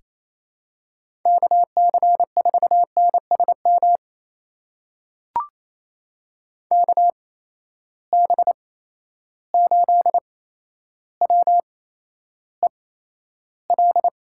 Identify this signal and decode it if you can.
Morse code, heard as a keyed tone, about 28 wpm, spaced as for 12 wpm, 700 Hz. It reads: KC4NSM KB8WEL
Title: USA Call Signs - 6 Characters - 8x Character Spacing - 28wpm